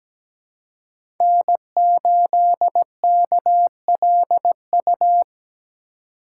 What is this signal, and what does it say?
Morse code, heard as a keyed tone, about 17 wpm, 700 Hz. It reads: N8KLU
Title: USA Call Signs - 5 Characters - 3x Character Spacing - 17wpm